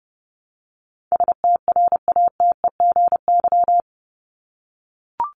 Morse code, heard as a keyed tone, about 30 wpm, 700 Hz. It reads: STRATEGY
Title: Top 501-750 Words 30wpm